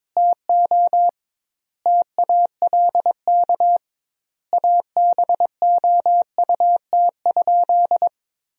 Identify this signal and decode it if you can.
Morse code, heard as a keyed tone, about 21 wpm, 700 Hz. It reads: TO TALK ABOUT?
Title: 2x Word-Spacing - Sentences from Top 400 Words - 22wpm